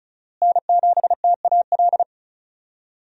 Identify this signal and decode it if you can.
Morse code, heard as a keyed tone, about 34 wpm, 700 Hz. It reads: N7TAL
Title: USA Call Signs - 5 Characters - Repeated 2x - 35wpm